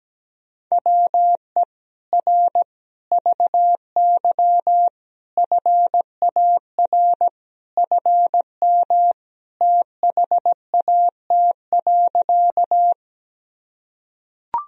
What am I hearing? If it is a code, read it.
Morse code, heard as a keyed tone, about 17 wpm, 700 Hz. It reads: WE R VY FAR FM THAT.
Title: Sentences from Top 300 Words and Abbreviations 17wpm